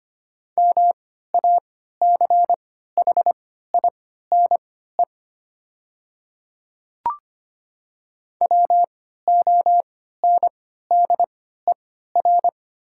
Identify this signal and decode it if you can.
Morse code, heard as a keyed tone, about 25 wpm, spaced as for 14 wpm, 700 Hz. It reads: MACHINE WONDER
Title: 3x Character Spacing - Top 500 Words - 25wpm